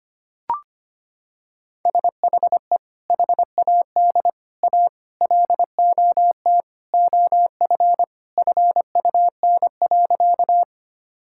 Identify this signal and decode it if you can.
Morse code, heard as a keyed tone, about 25 wpm, 700 Hz. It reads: SHE HAD A LOT OF FUN.
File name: practice / 25wpm / Sentences with 4-Word Phrases - 25wpm